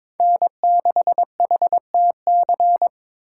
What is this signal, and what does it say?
Morse code, heard as a keyed tone, about 26 wpm, 700 Hz. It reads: N6HTC
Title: Speed Racing - USA 1x3 Call Signs 22wpm